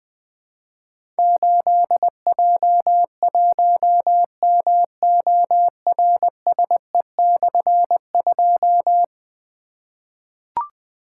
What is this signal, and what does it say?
Morse code, heard as a keyed tone, about 20 wpm, 700 Hz. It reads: 8J1MORSE/2
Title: Difficult Call Signs 20wpm